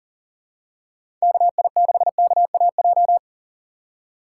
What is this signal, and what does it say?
Morse code, heard as a keyed tone, about 37 wpm, 700 Hz. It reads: KI6KAJ
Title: USA Call Signs - 6 Characters - 2x Character Spacing - 40wpm